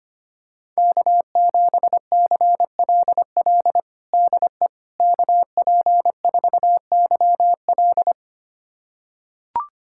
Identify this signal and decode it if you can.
Morse code, heard as a keyed tone, about 25 wpm, 700 Hz. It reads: K7CLL DE KP4YL
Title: USA Callsign DE Callsign - 5 Characters - 25wpm